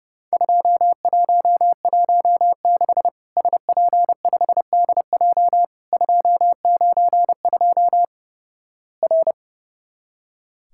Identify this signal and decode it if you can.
Morse code, heard as a keyed tone, about 30 wpm, 700 Hz. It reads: 2116 SP5DJ 292 R
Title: Worked All Europe DX Contest - QTC Exchange - 30wpm